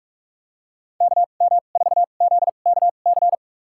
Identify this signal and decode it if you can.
Morse code, heard as a keyed tone, about 42 wpm, spaced as for 31 wpm, 700 Hz. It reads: KM4ZKC
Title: USA Call Signs - 2x Character Spacing - 45wpm